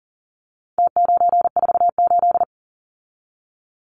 Morse code, heard as a keyed tone, about 40 wpm, 700 Hz. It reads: T948
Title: Numbers - Contest Style - 4-Digit 40wpm